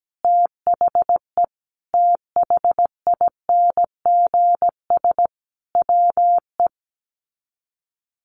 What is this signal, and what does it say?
Morse code, heard as a keyed tone, about 17 wpm, 700 Hz. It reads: THE THINGS WE
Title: Sets of 5 Words Encore 17wpm